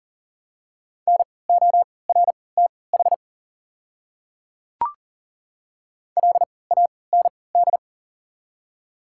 Morse code, heard as a keyed tone, about 40 wpm, spaced as for 23 wpm, 700 Hz. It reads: NORTH LAND
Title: 3x Character Spacing - Top 500 Words - 40wpm